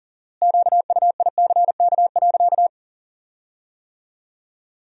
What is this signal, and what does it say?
Morse code, heard as a keyed tone, about 40 wpm, 700 Hz. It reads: QUICK.
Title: 4x Word-Spacing - Sentences from Top 500 Words - 40wpm